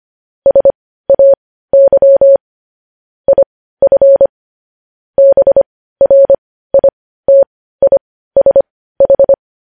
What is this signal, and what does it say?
Morse code, heard as a keyed tone, about 25 wpm, spaced as for 15 wpm, 550 Hz. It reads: SAY IF BRITISH
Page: news headlines in morse code at 15 wpm